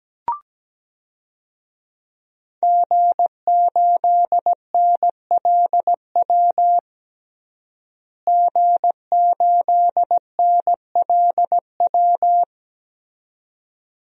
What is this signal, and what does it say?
Morse code, heard as a keyed tone, about 17 wpm, 700 Hz. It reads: G8NLW G8NLW
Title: DXCC-100 Call Signs - 5 Characters - Repeated 2x - 17wpm